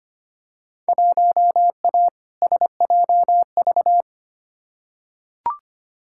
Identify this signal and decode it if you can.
Morse code, heard as a keyed tone, about 25 wpm, 700 Hz. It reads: 1A SJV
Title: Field Day Exchanges Encore - 25wpm